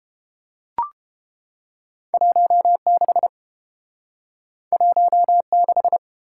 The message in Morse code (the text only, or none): 16 16